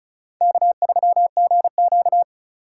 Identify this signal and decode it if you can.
Morse code, heard as a keyed tone, about 36 wpm, 700 Hz. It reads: K3GQ
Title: USA Call Signs - 4 Characters - 4x Character Spacing - 35wpm